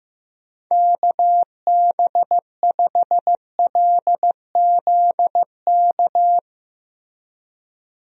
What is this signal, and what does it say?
Morse code, heard as a keyed tone, about 16 wpm, 700 Hz. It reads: KB5LZK
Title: USA Call Signs - 6 Characters - Repeated 2x - 15wpm